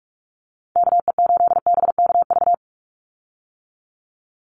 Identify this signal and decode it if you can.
Morse code, heard as a keyed tone, about 46 wpm, 700 Hz. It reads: KE8BKV
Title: USA Call Signs 45wpm